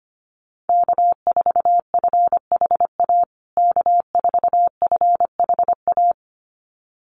Morse code, heard as a keyed tone, about 25 wpm, 700 Hz. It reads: K4FHA K4FHA
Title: USA Call Signs Repeated 2x 25wpm